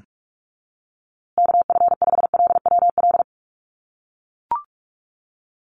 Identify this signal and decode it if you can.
Morse code, heard as a keyed tone, about 44 wpm, 700 Hz. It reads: KF5LWL
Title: USA Call Signs Repeated 2x 45wpm